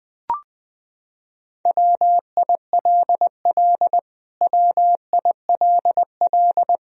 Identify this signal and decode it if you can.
Morse code, heard as a keyed tone, about 21 wpm, 700 Hz. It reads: WILL WILL
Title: Top 100 Words Repeated 20wpm